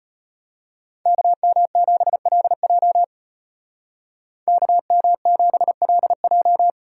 Morse code, heard as a keyed tone, about 33 wpm, 700 Hz. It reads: KM7LJ KM7LJ